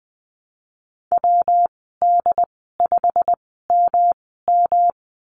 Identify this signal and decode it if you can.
Morse code, heard as a keyed tone, about 20 wpm, spaced as for 15 wpm, 700 Hz. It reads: WD5MM
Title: 122 Lesson 20g - Number 3 - Call Signs 20wpm